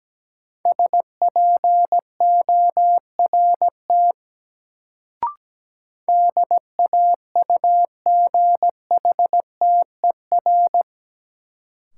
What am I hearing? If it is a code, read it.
Morse code, heard as a keyed tone, about 17 wpm, 700 Hz. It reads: SPORT DAUGHTER
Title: Sentences from Top 1500 Words - New Word Drill Alt - 17wpm